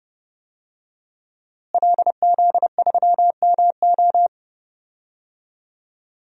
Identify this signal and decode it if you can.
Morse code, heard as a keyed tone, about 30 wpm, 700 Hz. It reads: LZ3MO
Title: DXCC-100 Call Signs - 5 Characters - Repeated 2x - 30wpm